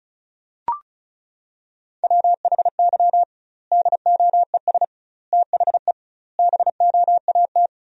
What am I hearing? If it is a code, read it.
Morse code, heard as a keyed tone, about 36 wpm, 700 Hz. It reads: WHY DOES THE BOAT
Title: 2x Word-Spacing - Sentences from Top 500 Words - 35wpm